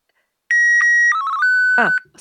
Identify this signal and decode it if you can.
SSTV (slow-scan television), Martin 1, with speech.